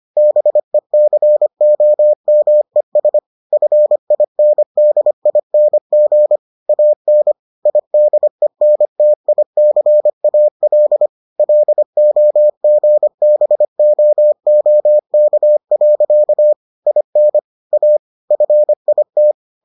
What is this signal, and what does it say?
Morse code, heard as a keyed tone, about 25 wpm, 600 Hz. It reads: BECOMES FINDING AN IDENTICAL LOGBOOK. IN A FIT